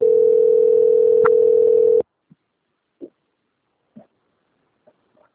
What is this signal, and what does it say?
Ringback tone (North America), 440 + 480 Hz.
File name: out- - - - - .20479